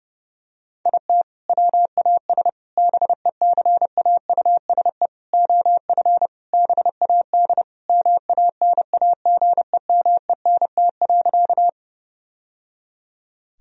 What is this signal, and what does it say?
Morse code, heard as a keyed tone, about 30 wpm, 700 Hz. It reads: IT WAS BECAUSE OF BAD MANAGEMENT.
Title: Sentences from Top 1000 Words - Review Part 4 - 30wpm